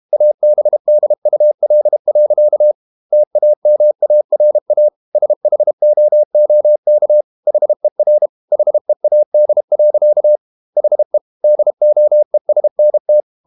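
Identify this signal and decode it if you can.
Morse code, heard as a keyed tone, about 32 wpm, 600 Hz. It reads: ABDUL. TAMARA SHOOK HER HEAD. HE DOESNT
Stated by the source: Morse code practice at 32 wpm from CW med Gnister, reading Ken Follett's novel Never in English.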